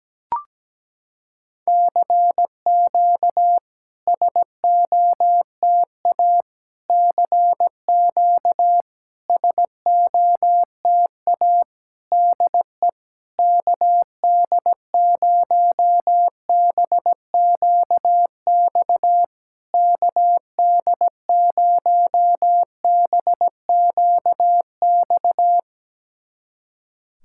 Morse code, heard as a keyed tone, about 17 wpm, 700 Hz. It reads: CQ SOTA CQ SOTA DE KD0BQX KD0BQX
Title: SOTA QSO Practice 17wpm